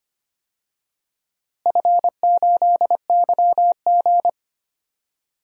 Morse code, heard as a keyed tone, about 25 wpm, 700 Hz. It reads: F8YG
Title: DXCC-100 Call Signs - 4 Characters - Repeated 3x - 25wpm